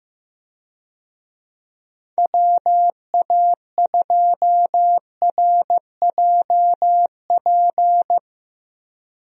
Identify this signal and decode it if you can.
Morse code, heard as a keyed tone, about 15 wpm, 700 Hz. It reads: WA2RJP
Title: USA Call Signs - 6 Characters - Repeated 2x - 15wpm